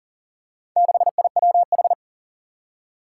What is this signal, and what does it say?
Morse code, heard as a keyed tone, about 40 wpm, 700 Hz. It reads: 6IWH
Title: Four Challenging Letter-Number Combinations 40wpm